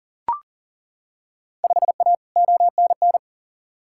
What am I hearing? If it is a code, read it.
Morse code, heard as a keyed tone, about 40 wpm, 700 Hz. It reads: 5A ONN
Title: Field Day Exchanges Encore - 40wpm